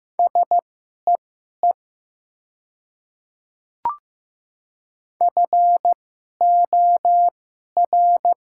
Morse code, heard as a keyed tone, about 15 wpm, spaced as for 11 wpm, 700 Hz. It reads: SEE FOR
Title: 2x Character Spacing - Top 100 Words - 15wpm